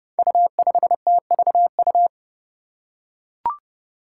Morse code, heard as a keyed tone, about 30 wpm, 700 Hz. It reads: U5TVU
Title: Five Challenging Letter-Number Combinations 30wpm